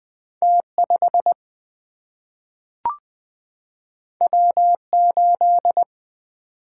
Morse code, heard as a keyed tone, about 20 wpm, 700 Hz. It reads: T5 W8
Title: Single-Letter Single-Number 20wpm